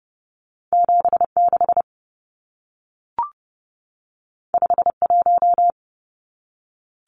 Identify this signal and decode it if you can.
Morse code, heard as a keyed tone, about 30 wpm, 700 Hz. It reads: 76 51
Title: Numbers - 2-Digit 30wpm